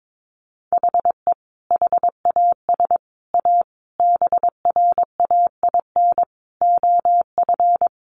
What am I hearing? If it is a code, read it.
Morse code, heard as a keyed tone, about 22 wpm, 700 Hz. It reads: HE HAS A BRAIN OF
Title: Sentences from Top 1100 Words 22wpm